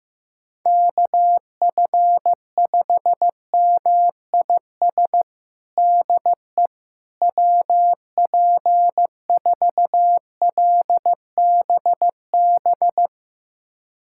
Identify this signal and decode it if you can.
Morse code, heard as a keyed tone, about 16 wpm, 700 Hz. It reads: KF5MIS DE WP4LBB
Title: USA Callsign DE Callsign - 6 Characters - 15wpm